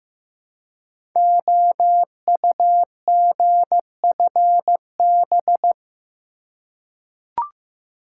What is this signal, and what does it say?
Morse code, heard as a keyed tone, about 15 wpm, 700 Hz. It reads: OUGFB
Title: Code Groups - 5 Letters - 15wpm